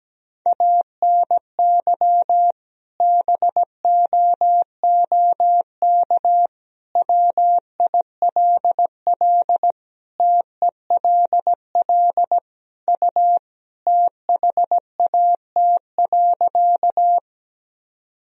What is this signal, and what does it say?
Morse code, heard as a keyed tone, about 17 wpm, 700 Hz. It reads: ANY BOOK WILL TELL U THAT.
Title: Sentences from Top 300 Words and Abbreviations 17wpm